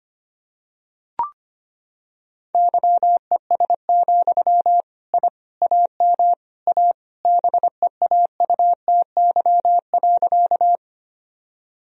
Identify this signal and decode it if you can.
Morse code, heard as a keyed tone, about 25 wpm, 700 Hz. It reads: YES, I AM A BEAUTY.
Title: Sentences from Top 1000 Words - Review Part 2 - 25wpm